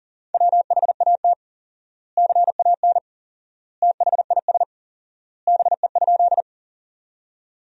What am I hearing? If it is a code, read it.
Morse code, heard as a keyed tone, about 40 wpm, 700 Hz. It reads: WHAT CAN THIS BE?
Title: 4x Word-Spacing - Sentences from Top 100 Words - 40wpm